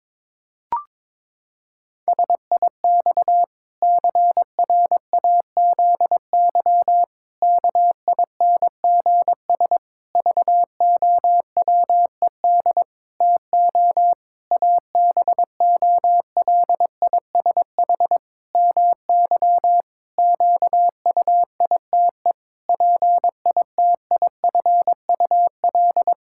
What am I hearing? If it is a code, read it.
Morse code, heard as a keyed tone, about 22 wpm, 700 Hz. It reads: SIX CRAZY KINGS VOWED TO ABOLISH MY QUITE PITIFUL
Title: Pangram Challenge v2 - 22wpm